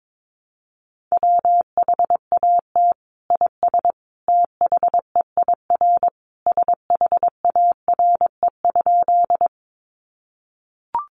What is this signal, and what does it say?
Morse code, heard as a keyed tone, about 22 wpm, 700 Hz. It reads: WHAT IS THEIR SHARE?
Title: Sentences from Top 900 Words 22wpm